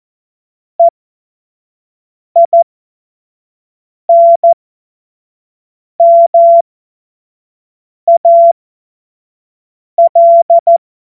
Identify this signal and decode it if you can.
Morse code, heard as a keyed tone, about 14 wpm, spaced as for 5 wpm, 680 Hz. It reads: EINMAL
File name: Morsekurs / Witze / Witz 19 März 2024 25 F70